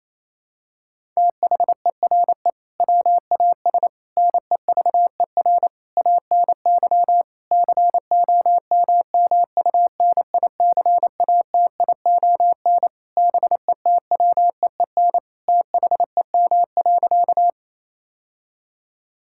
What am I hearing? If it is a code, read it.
Morse code, heard as a keyed tone, about 28 wpm, 700 Hz. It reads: THERE WAS NEVER ANY COMMUNICATION BETWEEN THEM.
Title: Sentences from Top 1000 Words - Review Part 6 - 28wpm